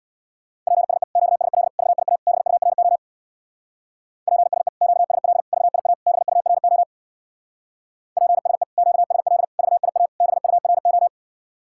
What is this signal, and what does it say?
Morse code, heard as a keyed tone, about 65 wpm, 700 Hz. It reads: PSE QSL VIA BURO PSE QSL VIA BURO PSE QSL VIA BURO